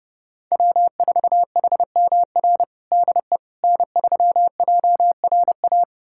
Morse code, heard as a keyed tone, about 30 wpm, 700 Hz. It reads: W4HMR DE N3JRA
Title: USA Callsign DE Callsign - 5 Characters - 30wpm